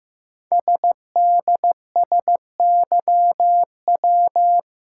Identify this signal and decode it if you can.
Morse code, heard as a keyed tone, about 15 wpm, 700 Hz. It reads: SDSYW
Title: Code Groups - 5 Letter:Numbers - 15wpm